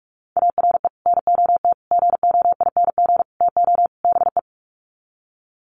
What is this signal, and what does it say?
Morse code, heard as a keyed tone, about 44 wpm, 700 Hz. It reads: ARE NOT GOING TO BE
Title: Sets of 5 Words Encore 45wpm